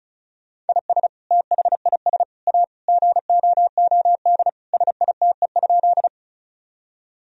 Morse code, heard as a keyed tone, about 36 wpm, 700 Hz. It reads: IS THIS A GOOD SITE?
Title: Sentences from Top 1000 Words - Review Part 4 - 35wpm